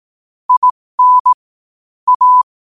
Morse code, heard as a keyed tone, about 18 wpm, spaced as for 15 wpm, 1000 Hz. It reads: IN A
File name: qotdfiles / 15wpm